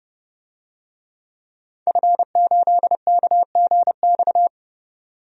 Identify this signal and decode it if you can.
Morse code, heard as a keyed tone, about 32 wpm, 700 Hz. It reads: F8KGX
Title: DXCC-100 Call Signs - 5 Characters - 30wpm